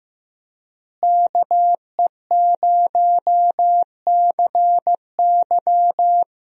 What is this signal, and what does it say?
Morse code, heard as a keyed tone, about 15 wpm, 700 Hz. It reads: KE0CY
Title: USA Call Signs - 5 Characters - Repeated 2x - 15wpm